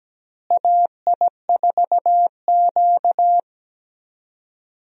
Morse code, heard as a keyed tone, about 17 wpm, 700 Hz. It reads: AI4Q
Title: USA Call Signs - 4 Characters - Repeated 2x - 17wpm